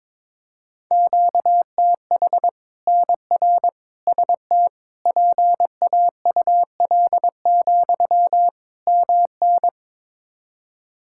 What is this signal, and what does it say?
Morse code, heard as a keyed tone, about 21 wpm, 700 Hz. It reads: QTH NR ST PAUL, MN
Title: K1USN SST QSO Practice 22wpm